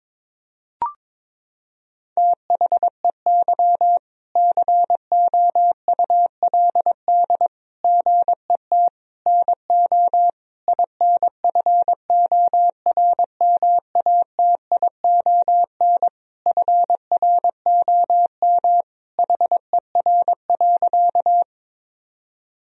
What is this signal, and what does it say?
Morse code, heard as a keyed tone, about 22 wpm, 700 Hz. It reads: THEY COULD GET NO INFORMATION FROM HER.
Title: Sentences from Top 1000 Words - Review Part 3 - 22wpm